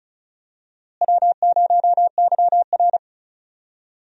Morse code, heard as a keyed tone, about 36 wpm, 700 Hz. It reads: W0YR
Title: USA Call Signs - 4 Characters - Repeated 2x - 35wpm